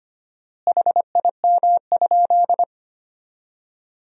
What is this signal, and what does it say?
Morse code, heard as a keyed tone, about 25 wpm, 700 Hz. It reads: HIM?
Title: 4x Word-Spacing - Sentences from Top 100 Words - 25wpm